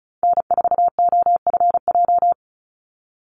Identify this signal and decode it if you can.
Morse code, heard as a keyed tone, about 34 wpm, 700 Hz. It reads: N4OFJ
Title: USA Call Signs 35wpm